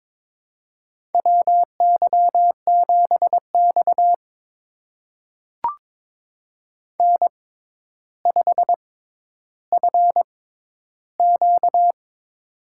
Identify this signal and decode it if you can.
Morse code, heard as a keyed tone, about 23 wpm, spaced as for 11 wpm, 700 Hz. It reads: WY7X N5FQ
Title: USA Call Signs - 4 Characters - 6x Character Spacing - 22wpm